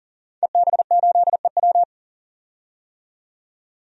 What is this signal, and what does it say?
Morse code, heard as a keyed tone, about 40 wpm, 700 Hz. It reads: EB8EW
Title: DXCC-100 Call Signs - 5 Characters - 40wpm